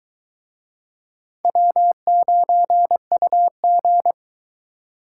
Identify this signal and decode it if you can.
Morse code, heard as a keyed tone, about 22 wpm, 700 Hz. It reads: W9UG